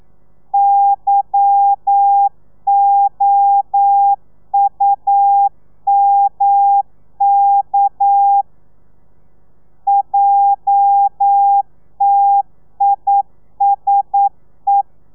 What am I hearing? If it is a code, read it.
Morse code, heard as a keyed tone, about 9 wpm, 800 Hz. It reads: YOUMK JTISE
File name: telegrafia / i2tfj / Lez11 40